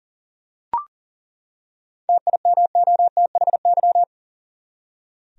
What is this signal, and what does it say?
Morse code, heard as a keyed tone, about 40 wpm, 700 Hz. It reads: TIMOTHY
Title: USA Common Names - 40wpm